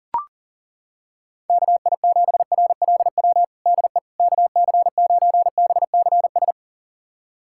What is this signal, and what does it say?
Morse code, heard as a keyed tone, about 37 wpm, 700 Hz. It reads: KI7RLW DE KC9BCS
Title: USA Callsign DE Callsign - 6 Characters - 40wpm